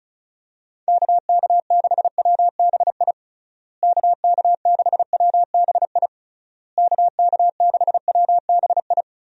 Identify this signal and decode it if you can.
Morse code, heard as a keyed tone, about 36 wpm, 700 Hz. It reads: KK6WBI KK6WBI KK6WBI
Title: USA Call Signs - 6 Characters - Repeated 3x - 35wpm